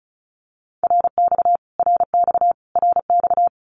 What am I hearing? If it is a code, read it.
Morse code, heard as a keyed tone, about 34 wpm, 700 Hz. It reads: RX RX RX
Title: Abbreviations Repeated 3x - 35wpm